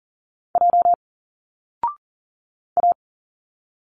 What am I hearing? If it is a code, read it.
Morse code, heard as a keyed tone, about 40 wpm, 700 Hz. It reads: J A
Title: Single Letters 40wpm